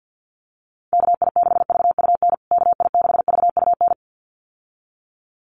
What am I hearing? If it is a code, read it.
Morse code, heard as a keyed tone, about 50 wpm, 700 Hz. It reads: KI6VUN KI6VUN
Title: USA Call Signs Repeated 2x 50wpm